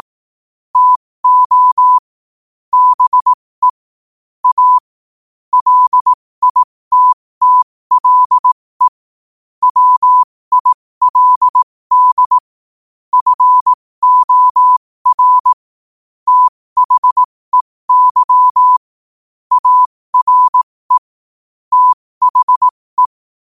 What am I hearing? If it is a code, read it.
Morse code, heard as a keyed tone, about 18 wpm, spaced as for 15 wpm, 1000 Hz. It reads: TO BE A LITTLE WILD FOR THEY ARE THE
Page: New quotes every day in morse code at 15 Words per minute.